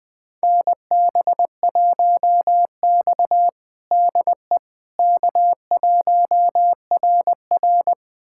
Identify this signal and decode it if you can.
Morse code, heard as a keyed tone, about 21 wpm, 700 Hz. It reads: NB1X DE K1RR
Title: USA Callsign DE Callsign - 4 Characters - 20wpm